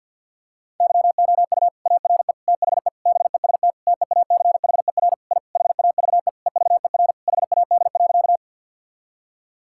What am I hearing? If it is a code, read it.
Morse code, heard as a keyed tone, about 50 wpm, 700 Hz. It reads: YOU ARE THE BEST TEACHER I HAVE EVER HAD.
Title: Sentences from Top 1000 Words - Review Part 6 - 50wpm